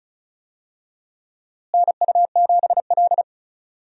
Morse code, heard as a keyed tone, about 34 wpm, 700 Hz. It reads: NU7L
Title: USA Call Signs - 4 Characters - Repeated 3x - 35wpm